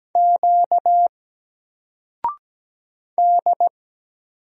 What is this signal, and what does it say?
Morse code, heard as a keyed tone, about 17 wpm, 700 Hz. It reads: Q D
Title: Single Letter-Number Alt 17wpm